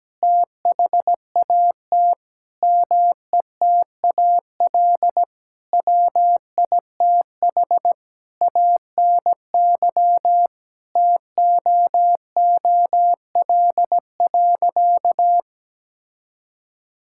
Morse code, heard as a keyed tone, about 17 wpm, 700 Hz. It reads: THAT METAL WITH ANY TOOL.
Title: Sentences from Top 1000 Words - Review Part 6 - 17wpm